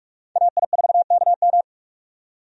Morse code, heard as a keyed tone, about 41 wpm, 700 Hz. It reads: AI4KM
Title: USA Call Signs - 5 Characters - 45wpm